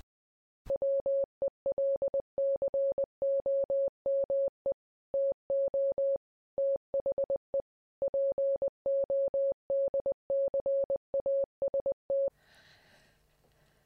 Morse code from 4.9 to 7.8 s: TO THE